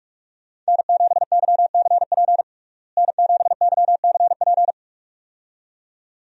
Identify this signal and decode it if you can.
Morse code, heard as a keyed tone, about 42 wpm, 700 Hz. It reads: N7YCP N7YCP
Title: USA Call Signs - 5 Characters - Repeated 2x - 45wpm